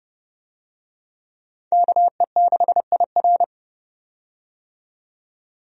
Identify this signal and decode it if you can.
Morse code, heard as a keyed tone, about 30 wpm, 700 Hz. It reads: KE6IR
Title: USA Call Signs - 5 Characters - 30wpm